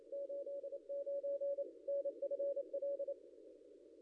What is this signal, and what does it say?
Morse code, heard as a keyed tone, about 23 wpm, 560 Hz. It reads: 89 NFL
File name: contests / ss-cw-k5tr-op-n5rz -0800